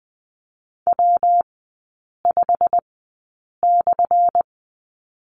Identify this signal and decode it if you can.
Morse code, heard as a keyed tone, about 20 wpm, spaced as for 8 wpm, 700 Hz. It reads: W5/
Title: 178 Lesson 29e - Slash - Three Characters 20wpm